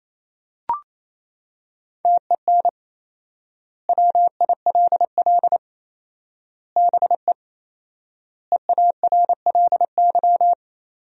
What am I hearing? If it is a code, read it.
Morse code, heard as a keyed tone, about 28 wpm, 700 Hz. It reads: TEN WILL BE EARLY
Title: 4x Word-Spacing - Sentences from Top 400 Words - 28wpm